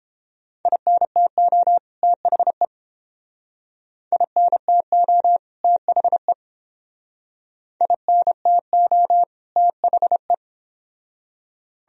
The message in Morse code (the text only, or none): INTO THE INTO THE INTO THE